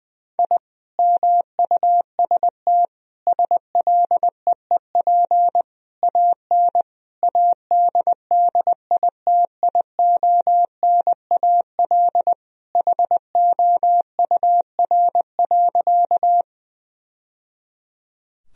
Morse code, heard as a keyed tone, about 21 wpm, 700 Hz. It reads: I MUST SLEEP AN ADDITIONAL HOUR.
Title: Sentences from Top 1000 Words - Review Part 4 - 20wpm